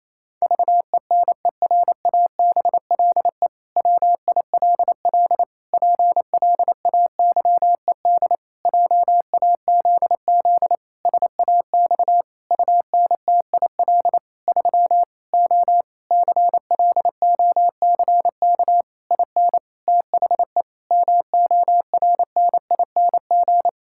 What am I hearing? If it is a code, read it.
Morse code, heard as a keyed tone, about 28 wpm, 700 Hz. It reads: VENERABLE WILL PLAYED JAZZ SAX UNTIL 3 O CLOCK IN THE MORNING
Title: Pangram Challenge v2 - 28wpm